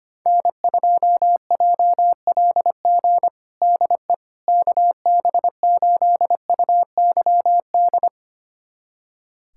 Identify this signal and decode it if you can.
Morse code, heard as a keyed tone, about 25 wpm, 700 Hz. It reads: N2JLG DE KB8UYD
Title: USA Callsign DE Callsign 25wpm